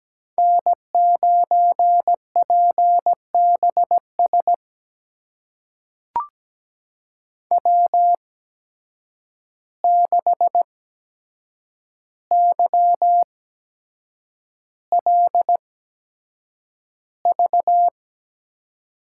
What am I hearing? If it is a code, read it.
Morse code, heard as a keyed tone, about 17 wpm, spaced as for 7 wpm, 700 Hz. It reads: N9PBS W6YLV
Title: USA Call Signs - 5 Characters - 8x Character Spacing - 17wpm